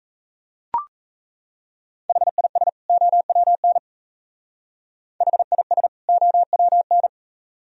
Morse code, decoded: HIS OWN HIS OWN

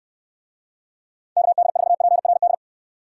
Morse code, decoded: KD4PRD

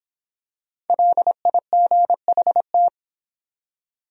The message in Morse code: LIGHT

